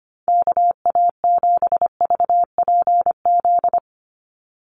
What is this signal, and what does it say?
Morse code, heard as a keyed tone, about 25 wpm, 700 Hz. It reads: KA7VPZ
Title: USA Call Signs 25wpm